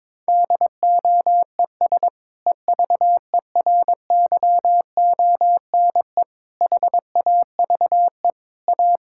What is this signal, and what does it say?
Morse code, heard as a keyed tone, about 22 wpm, 700 Hz. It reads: DOES EVERYONE HAVE A
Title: Sentences from Top 1000 Words - Review Part 6 - 22wpm